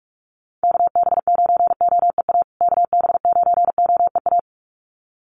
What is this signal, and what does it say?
Morse code, heard as a keyed tone, about 44 wpm, 690 Hz. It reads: KB9OEA KB9OEA
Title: USA Call Signs Repeated 2x 45wpm